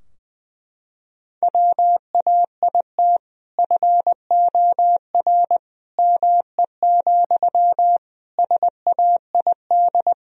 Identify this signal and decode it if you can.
Morse code, heard as a keyed tone, about 21 wpm, 700 Hz. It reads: WAIT FOR ME, SAID
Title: Sentences from Top 1000 Words - Review Part 4 - 20wpm